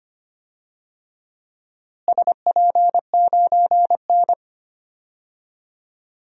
Morse code, heard as a keyed tone, about 25 wpm, 700 Hz. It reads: SP9N
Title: DXCC-100 Call Signs - 4 Characters - Repeated 2x - 25wpm